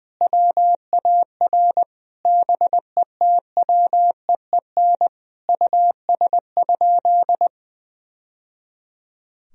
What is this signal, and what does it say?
Morse code, heard as a keyed tone, about 20 wpm, 700 Hz. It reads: WAR BETWEEN US?
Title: Sentences from Top 1000 Words - Review Part 2 - 20wpm